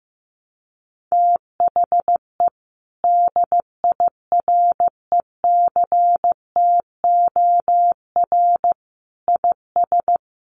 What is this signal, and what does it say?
Morse code, heard as a keyed tone, about 15 wpm, 700 Hz. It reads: THE DIRECTOR IS
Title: Sentences from Top 1300 Words 15wpm